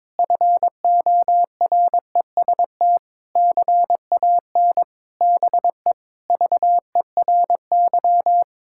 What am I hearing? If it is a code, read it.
Morse code, heard as a keyed tone, about 22 wpm, 700 Hz. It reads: FOREST CAN BE VERY
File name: Difficult Sentences with Hints - 22wpm